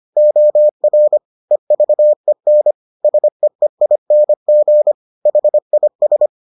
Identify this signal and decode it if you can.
Morse code, heard as a keyed tone, about 25 wpm, 600 Hz. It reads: OR EVEN SEEING HIS